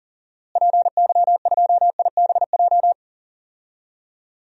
Morse code, heard as a keyed tone, about 40 wpm, 700 Hz. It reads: PY2IBJ
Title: DXCC-100 Call Signs - 6 Characters - Repeated 3x - 40wpm